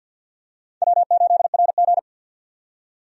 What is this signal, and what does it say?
Morse code, heard as a keyed tone, about 53 wpm, 700 Hz. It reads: W8RG
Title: USA Call Signs - 4 Characters - 50wpm